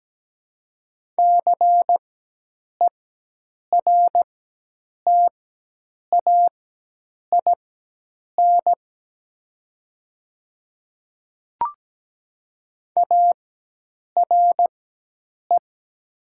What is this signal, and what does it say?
Morse code, heard as a keyed tone, about 17 wpm, spaced as for 8 wpm, 700 Hz. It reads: CERTAIN ARE
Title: 4x Character Spacing - Top 500 Words - 17wpm